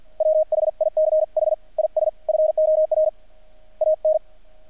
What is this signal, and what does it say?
Morse code, heard as a keyed tone, about 50 wpm, 620 Hz. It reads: WHICH IS FOR AN
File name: archived / CRSnet